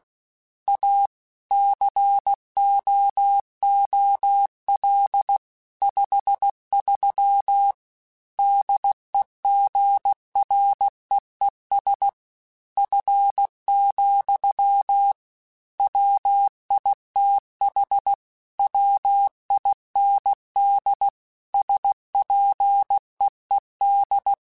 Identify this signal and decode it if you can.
Morse code, heard as a keyed tone, about 16 wpm, 800 Hz. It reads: A COOL 53 DEGREES F, WITH WIND SPEED
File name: Weather Sets / Weather Sets-mp3s / Weather 5 1 @15